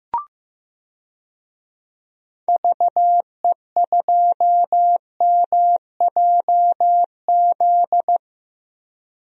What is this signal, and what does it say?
Morse code, heard as a keyed tone, about 15 wpm, 700 Hz. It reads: VE2MJZ
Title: DXCC-100 Call Signs - 6 Characters - Repeated 2x - 15wpm